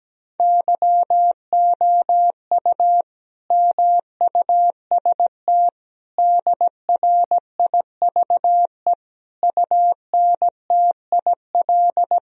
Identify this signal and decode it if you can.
Morse code, heard as a keyed tone, about 17 wpm, 700 Hz. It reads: YOU MUST DRIVE UNTIL
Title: Sentences from Top 1000 Words - Review Part 2 - 17wpm